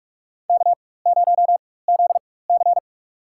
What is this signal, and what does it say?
Morse code, heard as a keyed tone, about 44 wpm, spaced as for 21 wpm, 690 Hz. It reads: K0ZC
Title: USA Call Signs - 4 Characters - 4x Character Spacing - 45wpm